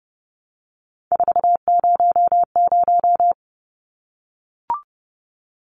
Morse code, heard as a keyed tone, about 30 wpm, 700 Hz. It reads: 400
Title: Numbers - 3-Digit 30wpm